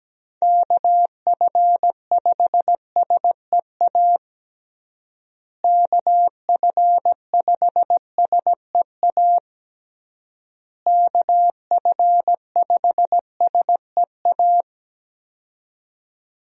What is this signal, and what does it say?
Morse code, heard as a keyed tone, about 17 wpm, 700 Hz. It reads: KF5SEA KF5SEA KF5SEA
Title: USA Call Signs - 6 Characters - Repeated 3x - 17wpm